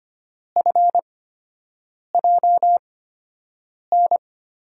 Morse code, heard as a keyed tone, about 25 wpm, spaced as for 7 wpm, 700 Hz. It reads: FJN
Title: USA Call Signs - 6 Characters - 8x Character Spacing - 25wpm